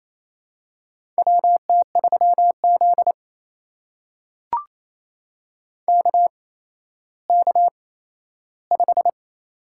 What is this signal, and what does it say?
Morse code, heard as a keyed tone, about 28 wpm, spaced as for 14 wpm, 700 Hz. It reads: WT3Z KK5
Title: USA Call Signs - 4 Characters - 8x Character Spacing - 28wpm